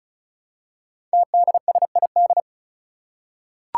Morse code, heard as a keyed tone, about 36 wpm, 700 Hz. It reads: TDSID
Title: Five Challenging Letter-Number Combinations 35wpm